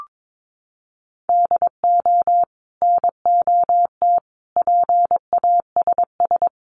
Morse code, heard as a keyed tone, about 22 wpm, 700 Hz. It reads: DO NOT PASS
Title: Sentences from Top 1100 Words 22wpm